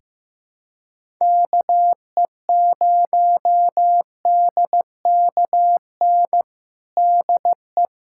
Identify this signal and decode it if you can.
Morse code, heard as a keyed tone, about 15 wpm, 700 Hz. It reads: KE0DKN DE
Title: USA Callsign DE Callsign - 6 Characters - 15wpm